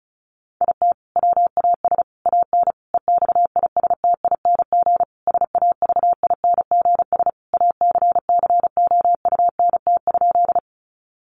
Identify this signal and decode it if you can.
Morse code, heard as a keyed tone, about 34 wpm, 700 Hz. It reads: IT WAS AN EXISTING SAVINGS ACCOUNT?
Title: Sentences from Top 1500 Words 35wpm